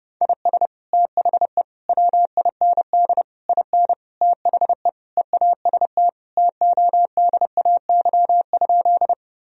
Morse code, heard as a keyed tone, about 30 wpm, 700 Hz. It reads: IS THE WIND IN THE EAST TODAY?
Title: Sentences from Top 1000 Words - Review Part 3 - 30wpm